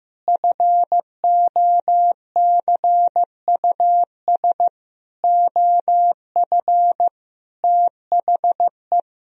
Morse code, heard as a keyed tone, about 15 wpm, 700 Hz. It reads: FOCUS OF THE
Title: Sentences from Top 1000 Words - Review Part 4 - 15wpm